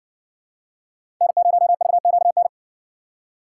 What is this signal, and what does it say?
Morse code, heard as a keyed tone, about 60 wpm, 700 Hz. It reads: N0VQN